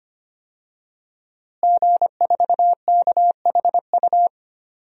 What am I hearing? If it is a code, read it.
Morse code, heard as a keyed tone, about 25 wpm, 700 Hz. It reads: G4KHU
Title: DXCC-100 Call Signs - 5 Characters - Repeated 3x - 25wpm